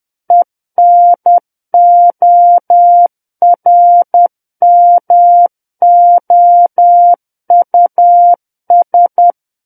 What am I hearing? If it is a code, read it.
Morse code, heard as a keyed tone, about 10 wpm, 700 Hz. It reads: ENORMOUS
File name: Book One / 01-Chapter 1 - 10 WPM